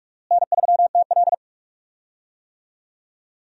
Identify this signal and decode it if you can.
Morse code, heard as a keyed tone, about 46 wpm, 700 Hz. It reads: N3TL
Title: USA Call Signs - 4 Characters - Repeated 2x - 45wpm